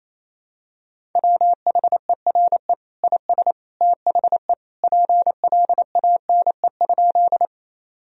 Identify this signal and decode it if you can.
Morse code, heard as a keyed tone, about 28 wpm, 700 Hz. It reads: WHERE IS THE PLANE?
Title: Sentences from Top 1000 Words - Review Part 3 - 28wpm